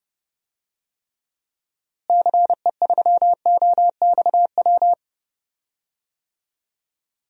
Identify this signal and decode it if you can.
Morse code, heard as a keyed tone, about 30 wpm, 700 Hz. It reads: CE3OXW
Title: DXCC-100 Call Signs - 6 Characters - Repeated 3x - 30wpm